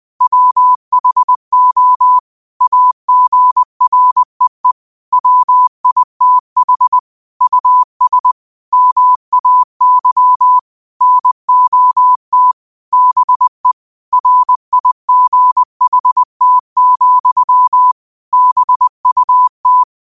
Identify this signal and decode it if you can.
Morse code, heard as a keyed tone, about 20 wpm, 1000 Hz. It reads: WHO AGREE WITH US MAY NOT BE RIGHT, BUT
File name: 20wpm